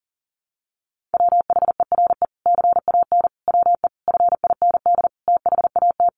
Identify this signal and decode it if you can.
Morse code, heard as a keyed tone, about 40 wpm, 700 Hz. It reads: WHERE CAN WE FIND THAT
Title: Sentences from Top 1500 Words 40wpm